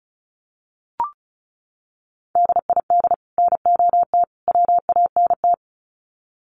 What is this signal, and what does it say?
Morse code, heard as a keyed tone, about 34 wpm, 700 Hz. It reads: DID NOT WANT
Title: Sets of 3 Words 35wpm